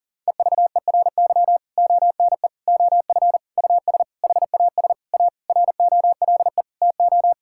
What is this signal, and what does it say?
Morse code, heard as a keyed tone, about 40 wpm, 700 Hz. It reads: EVERY ONE OF US HAS A ROLE TO
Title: Sentences from Top 1000 Words - Review Part 6 - 40wpm